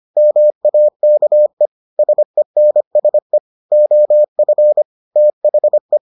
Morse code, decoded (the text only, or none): MAKE SENSE OF THE